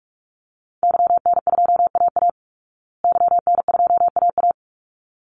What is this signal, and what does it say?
Morse code, heard as a keyed tone, about 44 wpm, 700 Hz. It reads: YN2AA YN2AA
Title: Call Signs Repeated 3x - 45wpm